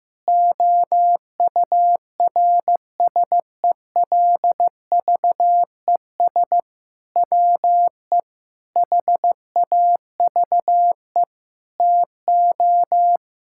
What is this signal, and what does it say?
Morse code, heard as a keyed tone, about 15 wpm, 700 Hz. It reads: OURSELVES WE HAVE TO
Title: Sentences from Top 1600 Words 15wpm